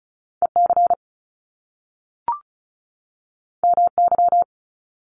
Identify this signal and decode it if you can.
Morse code, heard as a keyed tone, about 36 wpm, 700 Hz. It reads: EC MY
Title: Top Two Letter Combinations in Words 35wpm